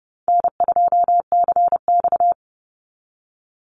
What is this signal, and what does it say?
Morse code, heard as a keyed tone, about 30 wpm, 700 Hz. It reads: N2CX
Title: Easy Call Signs Repeated 2x - 30wpm